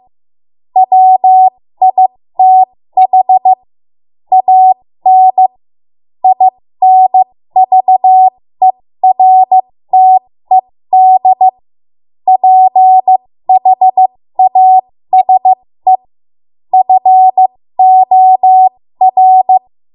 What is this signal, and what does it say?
Morse code, heard as a keyed tone, about 15 wpm, spaced as for 13 wpm, 750 Hz. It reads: WITH AN INVERTED PHASE FOR
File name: Morse / 13WPM